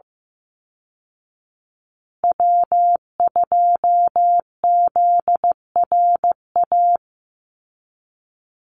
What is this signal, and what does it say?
Morse code, heard as a keyed tone, about 15 wpm, 700 Hz. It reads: W2ZRA
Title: Call Signs Repeated 2x - 15wpm